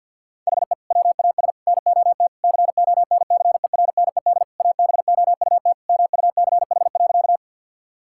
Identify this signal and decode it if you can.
Morse code, heard as a keyed tone, about 50 wpm, 700 Hz. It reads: HE WAS NOT CONCERNED ABOUT MUCH.
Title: Sentences from Top 1000 Words - Review Part 6 - 50wpm